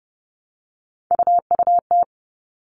Morse code, heard as a keyed tone, about 30 wpm, 700 Hz. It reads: UUT
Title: All Cut Numbers - 3-Digit 30wpm